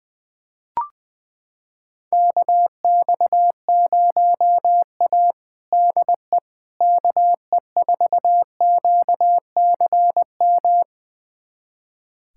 Morse code, heard as a keyed tone, about 20 wpm, 700 Hz. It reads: KX0A DE KE4QCM
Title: USA Callsign DE Callsign 20wpm